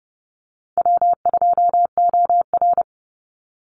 Morse code, heard as a keyed tone, about 30 wpm, 700 Hz. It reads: W2OR
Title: Easy Call Signs - 30wpm